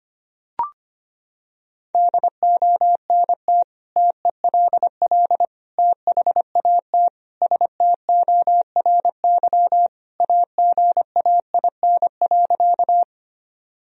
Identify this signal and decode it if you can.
Morse code, heard as a keyed tone, about 25 wpm, 700 Hz. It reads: DONT TELL THAT STORY AGAIN.
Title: Sentences from Top 1000 Words - Review Part 1 - 25wpm